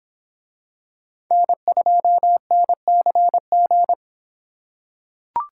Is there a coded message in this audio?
Yes: Morse code from 1.3 to 3.9 s, heard as a keyed tone, about 23 wpm, 700 Hz. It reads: N2NCG